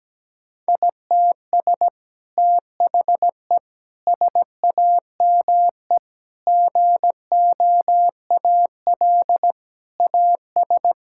Morse code, heard as a keyed tone, about 17 wpm, 700 Hz. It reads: ITS THE SAME GOAL AS
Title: Sentences from Top 1000 Words - Review Part 5 - 17wpm